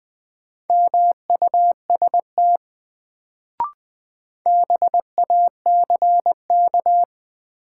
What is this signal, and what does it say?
Morse code, heard as a keyed tone, about 20 wpm, 700 Hz. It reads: MUST BACK
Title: Sets of 2 Words - New Word Drill - 20wpm